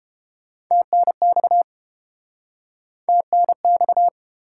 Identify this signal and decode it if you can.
Morse code, heard as a keyed tone, about 29 wpm, 700 Hz. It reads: TNX TNX